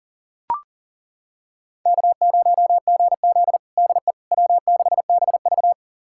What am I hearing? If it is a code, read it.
Morse code, heard as a keyed tone, about 40 wpm, 700 Hz. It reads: K0GZ DE W6BV
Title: USA Callsign DE Callsign - 4 Characters - 40wpm